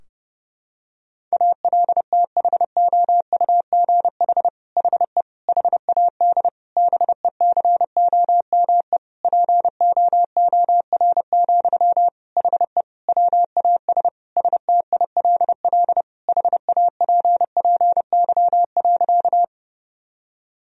Morse code, heard as a keyed tone, about 30 wpm, 700 Hz. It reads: ALTHOUGH HE HAD BECOME POOR, HE WAS STILL HAPPY.
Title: Sentences from Top 1000 Words - Review Part 3 - 30wpm